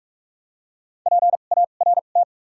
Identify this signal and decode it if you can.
Morse code, heard as a keyed tone, about 46 wpm, spaced as for 33 wpm, 700 Hz. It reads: PART
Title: 2x Character Spacing - Top 200 Words - 45wpm